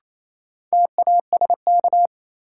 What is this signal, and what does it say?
Morse code, heard as a keyed tone, about 28 wpm, 700 Hz. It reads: TASK
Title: Sentences from Top 1000 Words - Review Part 7 - New Word Drill - 28wpm